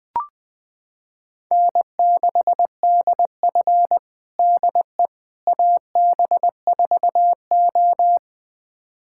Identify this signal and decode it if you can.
Morse code, heard as a keyed tone, about 20 wpm, 700 Hz. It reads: N6DF DE AB4O
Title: USA Callsign DE Callsign - 4 Characters - 20wpm